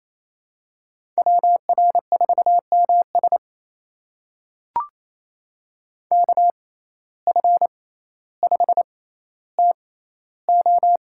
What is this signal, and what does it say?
Morse code, heard as a keyed tone, about 28 wpm, spaced as for 14 wpm, 700 Hz. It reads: WR4MS KF5TO
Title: USA Call Signs - 5 Characters - 6x Character Spacing - 28wpm